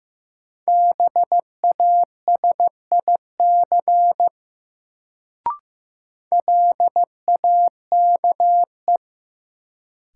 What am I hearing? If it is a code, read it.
Morse code, heard as a keyed tone, about 16 wpm, 700 Hz. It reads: BASIC LAKE
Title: Sentences from Top 1000 Words - Review Part 3 - New Word Drill - 15wpm